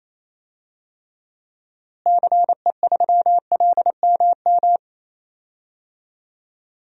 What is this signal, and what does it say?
Morse code, heard as a keyed tone, about 28 wpm, 700 Hz. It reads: CE3LMM
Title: DXCC-100 Call Signs - 6 Characters - Repeated 3x - 28wpm